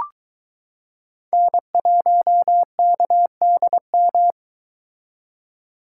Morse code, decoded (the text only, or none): N1KDM